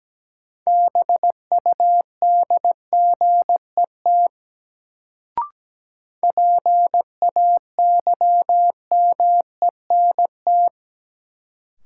Morse code, heard as a keyed tone, about 17 wpm, 700 Hz. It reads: BUDGET PAYMENT
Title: Sentences from Top 1200 Words - New Word Drill - 17wpm